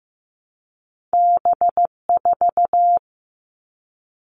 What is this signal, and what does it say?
Morse code, heard as a keyed tone, about 15 wpm, 700 Hz. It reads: B4
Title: Two Challenging Letter-Number Combinations 15wpm